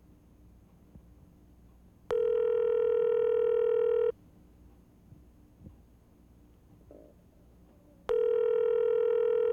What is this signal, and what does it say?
Ringback tone (North America), 440 + 480 Hz.
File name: nabor-nomera-i-vyzov-gudki-st-f